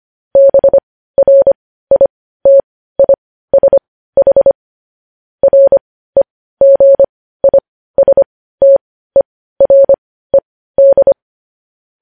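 Morse code, heard as a keyed tone, about 25 wpm, spaced as for 15 wpm, 550 Hz. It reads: BRITISH REGISTERED
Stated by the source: Morse news headlines sent at 15 WPM, beginning with vvv.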